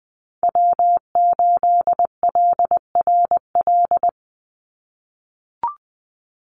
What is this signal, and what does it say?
Morse code, heard as a keyed tone, about 19 wpm, 700 Hz. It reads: W8LRL
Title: USA Call Signs 20wpm